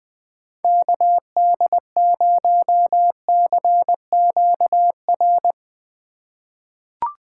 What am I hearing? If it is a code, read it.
Morse code, heard as a keyed tone, about 21 wpm, 700 Hz. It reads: KD0CQR
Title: USA Call Signs - 6 Characters - 3x Character Spacing - 20wpm